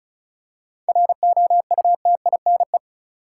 Morse code, heard as a keyed tone, about 34 wpm, 700 Hz. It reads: ROUTINE
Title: Seven Letter Words - 35wpm